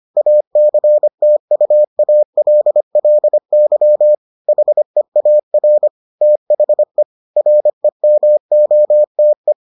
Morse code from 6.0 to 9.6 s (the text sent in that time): THE REMOTE